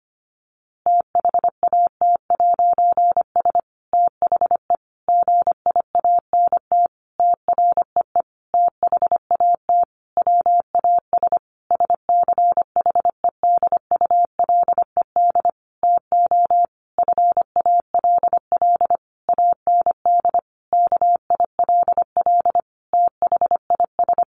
Morse code, heard as a keyed tone, about 25 wpm, 700 Hz. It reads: THAT'S THE GIANT TREE THAT WAS SCHEDULED TO FALL AND KILL THIS
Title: A Sound Of Thunder 25wpm